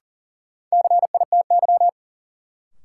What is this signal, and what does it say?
Morse code, heard as a keyed tone, about 40 wpm, 700 Hz. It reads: CITY
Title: Sentences from Top 300 Words - New Word Drill - 40wpm